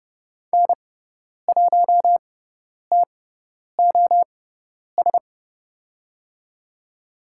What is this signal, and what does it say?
Morse code, heard as a keyed tone, about 30 wpm, spaced as for 10 wpm, 700 Hz. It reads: N1TOS
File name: Lesson 16g - Letter c - Call Signs 30wpm